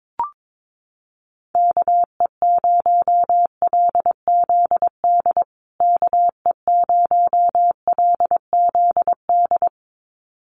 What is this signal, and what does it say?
Morse code, heard as a keyed tone, about 22 wpm, 700 Hz. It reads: KE0LZD KE0LZD
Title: USA Call Signs Repeated 2x 22wpm